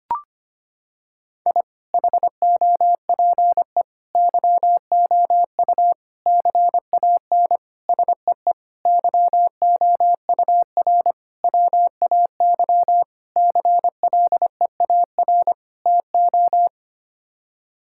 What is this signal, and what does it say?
Morse code, heard as a keyed tone, about 25 wpm, 700 Hz. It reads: I HOPE YOU CAN SEE YOUR WAY CLEAR TO
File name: Sets of 9 Words - 25wpm